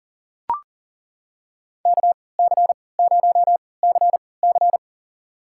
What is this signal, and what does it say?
Morse code, heard as a keyed tone, about 40 wpm, spaced as for 23 wpm, 700 Hz. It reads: KC0CC
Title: USA Call Signs - 5 Characters - 3x Character Spacing - 40wpm